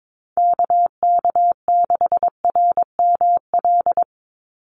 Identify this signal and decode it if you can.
Morse code, heard as a keyed tone, about 22 wpm, 700 Hz. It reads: KK6RML
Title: USA Call Signs 22wpm